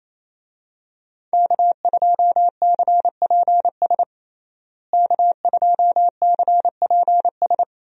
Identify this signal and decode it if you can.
Morse code, heard as a keyed tone, about 27 wpm, 700 Hz. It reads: K2CPS K2CPS
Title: USA Call Signs - 5 Characters - Repeated 3x - 28wpm